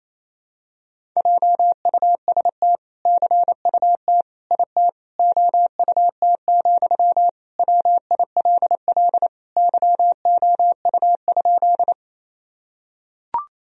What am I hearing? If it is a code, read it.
Morse code, heard as a keyed tone, about 28 wpm, 700 Hz. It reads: JUST CUT IT OUT, WILL YOU?
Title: Sentences from Top 1000 Words - Review Part 1 - 28wpm